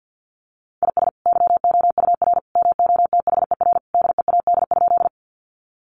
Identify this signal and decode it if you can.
Morse code, heard as a keyed tone, about 50 wpm, 700 Hz. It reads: IS YOUR MOTHER DEAD?
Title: Sentences from Top 1500 Words 50wpm